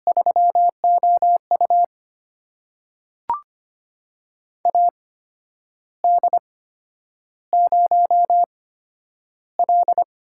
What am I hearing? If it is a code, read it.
Morse code, heard as a keyed tone, about 25 wpm, spaced as for 10 wpm, 700 Hz. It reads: 3OU AD0L
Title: USA Call Signs - 4 Characters - 8x Character Spacing - 25wpm